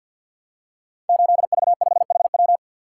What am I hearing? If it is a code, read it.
Morse code, heard as a keyed tone, about 53 wpm, 700 Hz. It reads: 8V5HW
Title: Code Groups - 5 Letter:Numbers - 50wpm